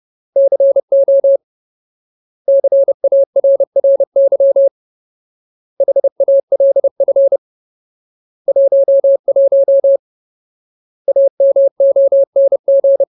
Morse code, heard as a keyed tone, about 30 wpm, 550 Hz. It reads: CO CARRY HALF 11 AMONG
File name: Ultimate Word List / 30WPM Ultimate